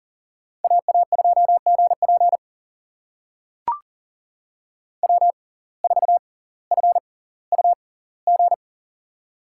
Morse code, heard as a keyed tone, about 40 wpm, spaced as for 20 wpm, 700 Hz. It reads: AA2GP W4FUG
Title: USA Call Signs - 5 Characters - 6x Character Spacing - 40wpm